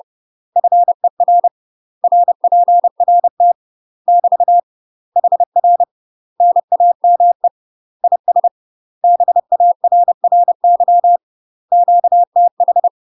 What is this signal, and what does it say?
Morse code, heard as a keyed tone, about 30 wpm, 700 Hz. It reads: FER RPRT <BT> HR NAME IS BARRY QTH
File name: UK MASTER QSOs TRAINING / 30WPM / UK MASTER QSOs 0029